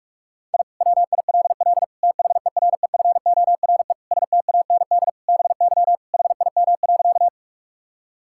Morse code, heard as a keyed tone, about 46 wpm, 700 Hz. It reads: I WILL THEREFORE STAND BY HIM.
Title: Sentences from Top 1000 Words - Review Part 5 - 45wpm